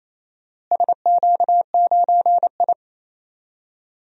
Morse code, heard as a keyed tone, about 28 wpm, 700 Hz. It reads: SQ9I
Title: DXCC-100 Call Signs - 4 Characters - Repeated 2x - 28wpm